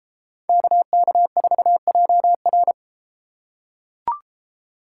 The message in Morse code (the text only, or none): KK4JR